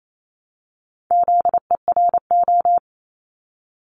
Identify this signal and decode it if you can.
Morse code, heard as a keyed tone, about 28 wpm, 700 Hz. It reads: ZERO
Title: Multi-letter Phonemes Repeated 3x 28wpm